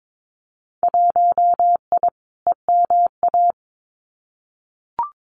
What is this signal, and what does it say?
Morse code, heard as a keyed tone, about 22 wpm, 700 Hz. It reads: 1I EMA
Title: Winter Field Day Exchanges Encore 22wpm